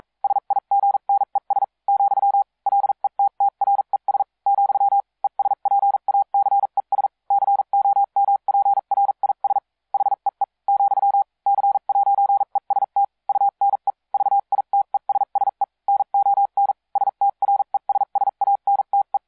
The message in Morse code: SIGNES , LETTRES , ESPACES COMPRIS HEE , C'EST UNE VITESSE NON STRESSANTE